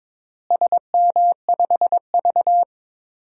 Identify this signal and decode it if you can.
Morse code, heard as a keyed tone, about 22 wpm, 700 Hz. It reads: SM5V
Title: DXCC-100 Call Signs - 4 Characters - Repeated 2x - 22wpm